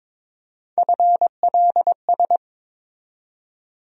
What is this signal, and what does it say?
Morse code, heard as a keyed tone, about 22 wpm, 700 Hz. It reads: FLS
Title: USA Call Sign Postfixes - 3 Letters 22wpm